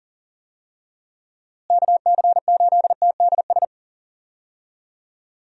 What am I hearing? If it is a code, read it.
Morse code, heard as a keyed tone, about 40 wpm, 700 Hz. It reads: KC8TDS
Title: USA Call Signs - 6 Characters - Repeated 3x - 40wpm